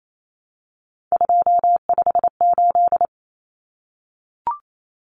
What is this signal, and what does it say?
Morse code, heard as a keyed tone, about 28 wpm, 700 Hz. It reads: 258
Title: Numbers - 3-Digit 28wpm